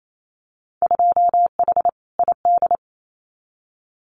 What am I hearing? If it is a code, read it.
Morse code, heard as a keyed tone, about 28 wpm, 700 Hz. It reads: 2H ID
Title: Winter Field Day Exchanges 28wpm